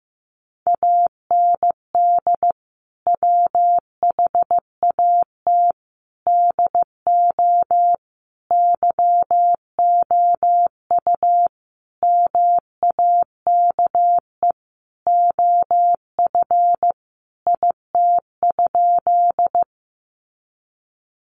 Morse code, heard as a keyed tone, about 15 wpm, 700 Hz. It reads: AND WHAT DO YOU MAKE OF IT?
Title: Sentences from Top 100 Words 15wpm